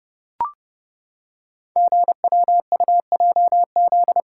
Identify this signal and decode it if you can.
Morse code, heard as a keyed tone, about 30 wpm, 700 Hz. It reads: GWUJZ
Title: Code Groups - 5 Letters - 30wpm